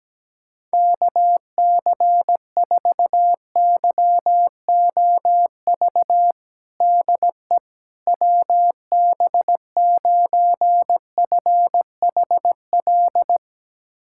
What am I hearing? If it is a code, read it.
Morse code, heard as a keyed tone, about 17 wpm, 700 Hz. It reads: KC4YOV DE WB9FHL
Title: USA Callsign DE Callsign - 6 Characters - 17wpm